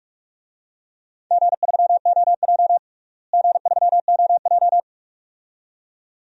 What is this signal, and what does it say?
Morse code, heard as a keyed tone, about 45 wpm, 700 Hz. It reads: G3OJ G3OJ
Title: DXCC-100 Call Signs - 4 Characters - Repeated 2x - 45wpm